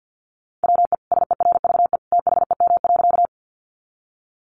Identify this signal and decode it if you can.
Morse code, heard as a keyed tone, about 50 wpm, 700 Hz. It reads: WE SERVE THEM.
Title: Sentences from Top 400 Words 50wpm